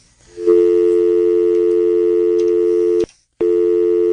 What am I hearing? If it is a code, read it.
Dial tone (North America), 350 + 440 Hz.